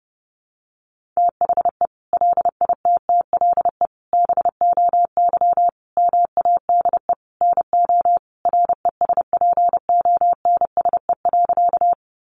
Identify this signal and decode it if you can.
Morse code, heard as a keyed tone, about 30 wpm, 700 Hz. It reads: THE LITTLE BOY MADE NO RESPONSE.
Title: Sentences from Top 1400 Words 30wpm